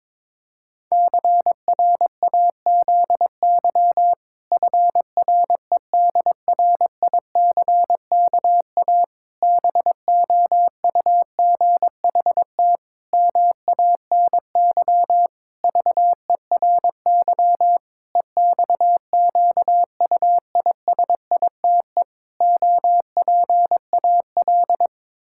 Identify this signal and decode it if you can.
Morse code, heard as a keyed tone, about 22 wpm, 700 Hz. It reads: CRAZY FREDRICKA BOUGHT MANY VERY EXQUISITE OPAL
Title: Pangram Challenge v2 - 22wpm